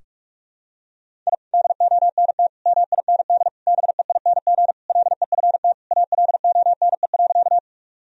Morse code, heard as a keyed tone, about 46 wpm, 700 Hz. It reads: I DONT MIND BEING LEFT ALONE.
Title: Sentences from Top 1000 Words - Review Part 3 - 45wpm